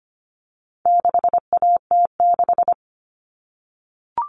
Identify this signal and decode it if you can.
Morse code, heard as a keyed tone, about 25 wpm, 700 Hz. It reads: 6AT6
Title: All Cut Numbers - 4-Digit 25wpm